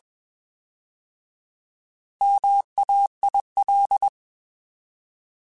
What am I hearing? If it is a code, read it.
Morse code, heard as a keyed tone, about 21 wpm, 800 Hz. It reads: MAIL
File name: Alphabetical / Alphabetical mp3s / Ms @20